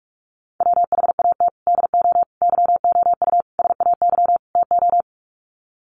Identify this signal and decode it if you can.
Morse code, heard as a keyed tone, about 44 wpm, 700 Hz. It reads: WHAT DO YOU SAY TO
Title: Sets of 5 Words 45wpm